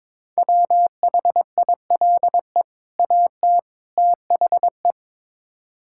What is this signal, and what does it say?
Morse code, heard as a keyed tone, about 22 wpm, 700 Hz. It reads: WHILE AT THE
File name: practice / 22wpm / Build A Phrase - 5-Word Phrases - 22wpm